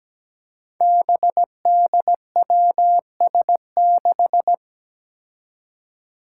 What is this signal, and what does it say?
Morse code, heard as a keyed tone, about 17 wpm, 700 Hz. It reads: BDWS6
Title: Five Challenging Letter-Number Combinations 17wpm